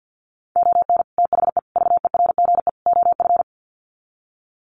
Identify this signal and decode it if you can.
Morse code, heard as a keyed tone, about 50 wpm, 700 Hz. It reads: ON THE VERGE OF
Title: Sets of 4 Words 50wpm